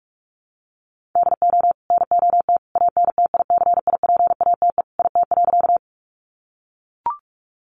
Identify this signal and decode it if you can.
Morse code, heard as a keyed tone, about 44 wpm, 700 Hz. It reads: DO NOT ANTICIPATE IT.
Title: Sentences from Top 1500 Words 45wpm